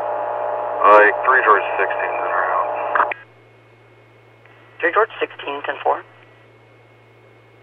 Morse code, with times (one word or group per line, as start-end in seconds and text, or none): none